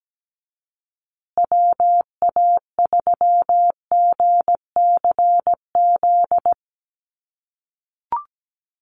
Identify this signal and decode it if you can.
Morse code, heard as a keyed tone, about 17 wpm, 700 Hz. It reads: WA3GCZ
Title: USA Call Signs Repeated 3x 17wpm